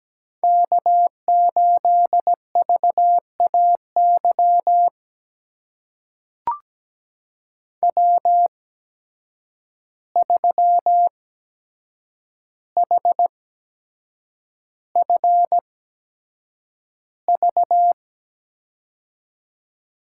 Morse code, heard as a keyed tone, about 17 wpm, spaced as for 7 wpm, 700 Hz. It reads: K8VAY W3HFV
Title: USA Call Signs - 5 Characters - 8x Character Spacing - 17wpm